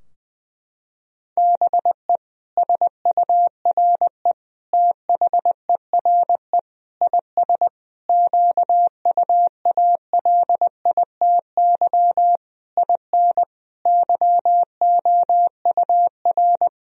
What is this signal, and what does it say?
Morse code, heard as a keyed tone, about 20 wpm, 700 Hz. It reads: BE SURE THERE IS QUALITY IN YOUR
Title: Sentences from Top 1000 Words - Review Part 4 - 20wpm